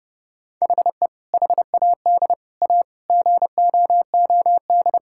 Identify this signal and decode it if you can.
Morse code, heard as a keyed tone, about 30 wpm, 700 Hz. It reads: HE HAD A GOOD
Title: Sentences from Top 1000 Words - Review Part 6 - 30wpm